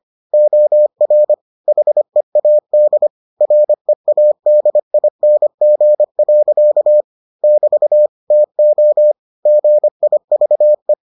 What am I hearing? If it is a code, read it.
Morse code, heard as a keyed tone, about 25 wpm, 600 Hz. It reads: OR HEAD READING. <BT> TO GIVE